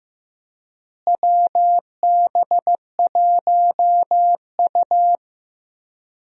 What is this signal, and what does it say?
Morse code, heard as a keyed tone, about 15 wpm, 700 Hz. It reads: WB1U
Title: USA Call Signs - 4 Characters - Repeated 3x - 15wpm